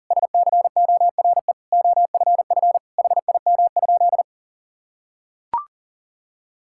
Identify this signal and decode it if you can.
Morse code, heard as a keyed tone, about 40 wpm, 700 Hz. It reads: SCORE OFF HIM?
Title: Sentences from Top 1600 Words 40wpm